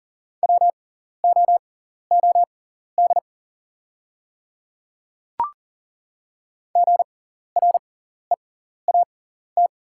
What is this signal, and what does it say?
Morse code, heard as a keyed tone, about 37 wpm, spaced as for 14 wpm, 700 Hz. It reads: WOOD GREAT
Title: 6x Character Spacing - Top 500 Words - 40wpm